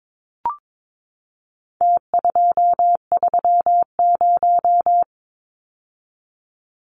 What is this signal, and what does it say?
Morse code, heard as a keyed tone, about 22 wpm, 700 Hz. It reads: T230
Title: Numbers - Contest Style - 4-Digit 22wpm